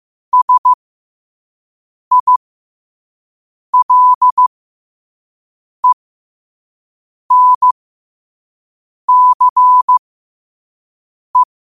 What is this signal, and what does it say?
Morse code, heard as a keyed tone, about 15 wpm, spaced as for 5 wpm, 1000 Hz. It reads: SILENCE
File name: qotdfiles / 5wpm